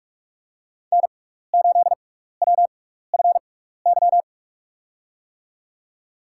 Morse code, heard as a keyed tone, about 46 wpm, spaced as for 15 wpm, 700 Hz. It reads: N8WFY
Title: USA Call Signs - 5 Characters - 6x Character Spacing - 45wpm